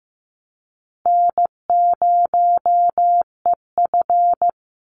Morse code, heard as a keyed tone, about 15 wpm, 700 Hz. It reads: N0EF
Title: Easy Call Signs Repeated 2x - 15wpm